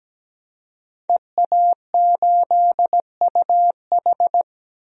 Morse code, heard as a keyed tone, about 17 wpm, 700 Hz. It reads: EA8UH
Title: DXCC-100 Call Signs - 5 Characters - Repeated 2x - 17wpm